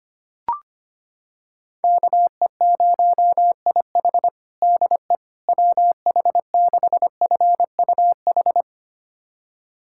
Morse code, heard as a keyed tone, about 25 wpm, 700 Hz. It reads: KE0IH DE WH6FUH
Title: USA Callsign DE Callsign 25wpm